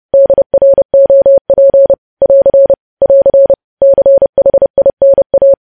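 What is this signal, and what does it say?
Morse code, heard as a keyed tone, about 30 wpm, 550 Hz. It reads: DROP <AR> <AR> CHINA
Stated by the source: Morse code news headlines sent at 30 words per minute.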